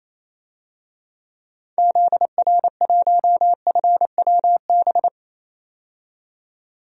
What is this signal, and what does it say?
Morse code, heard as a keyed tone, about 28 wpm, 700 Hz. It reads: ZR1FWB
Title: DXCC-100 Call Signs - 6 Characters - 28wpm